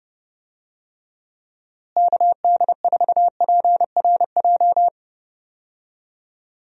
Morse code, heard as a keyed tone, about 30 wpm, 700 Hz. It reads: KD4PRJ
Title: USA Call Signs - 6 Characters - Repeated 2x - 30wpm